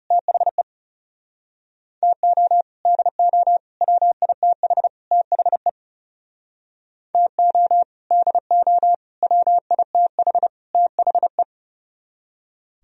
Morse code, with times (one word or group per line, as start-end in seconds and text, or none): 0.1-0.6 THE
2.0-2.6 TO
2.9-3.6 DO
3.8-4.9 WITH
5.1-5.7 THE
7.1-7.8 TO
8.1-8.9 DO
9.2-10.5 WITH
10.7-11.4 THE